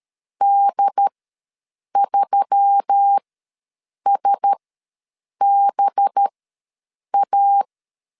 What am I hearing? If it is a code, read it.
Morse code, heard as a keyed tone, about 13 wpm, spaced as for 7 wpm, 800 Hz. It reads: D3SBA
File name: 5 LB ws 12wpm